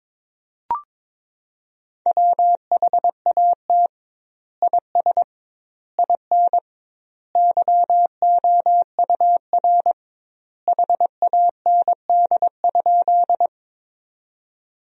Morse code, heard as a keyed tone, about 22 wpm, 700 Hz. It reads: WHAT IS IN YOUR HAND?
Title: 2x Word-Spacing - Sentences from Top 200 Words - 22wpm